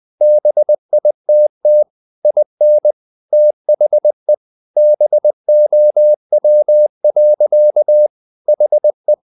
Morse code, heard as a keyed tone, about 20 wpm, 600 Hz. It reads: BITT IN THE BOW. HE
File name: the old man and the sea part 2 20wpm